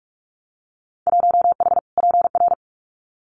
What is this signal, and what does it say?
Morse code, heard as a keyed tone, about 46 wpm, 700 Hz. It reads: 1H PR
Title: Winter Field Day Exchanges Encore 45wpm